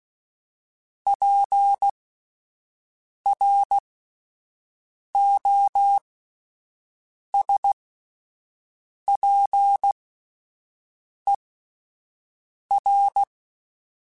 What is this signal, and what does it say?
Morse code, heard as a keyed tone, about 16 wpm, spaced as for 5 wpm, 800 Hz. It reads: PROSPER